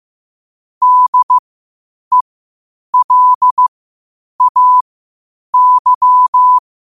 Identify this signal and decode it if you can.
Morse code, heard as a keyed tone, about 15 wpm, spaced as for 8 wpm, 1000 Hz. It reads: DELAY
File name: qotdfiles / 8wpm